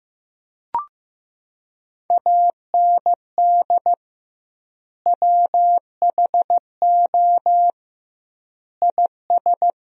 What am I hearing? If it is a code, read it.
Morse code, heard as a keyed tone, about 16 wpm, 700 Hz. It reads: AND WHO IS
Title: 2x Word-Spacing - Sentences from Top 100 Words - 15wpm